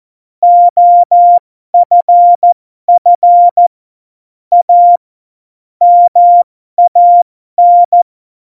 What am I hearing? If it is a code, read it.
Morse code, heard as a keyed tone, about 14 wpm, spaced as for 12 wpm, 700 Hz. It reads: OFF A MAN
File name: Morsekurs / Witze